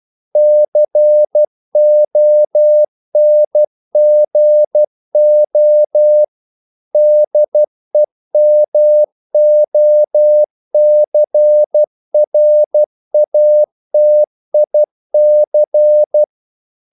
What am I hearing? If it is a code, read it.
Morse code, heard as a keyed tone, about 12 wpm, 600 Hz. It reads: CONGO DEMOCRATIC
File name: CW / Practice / NationalCapitals12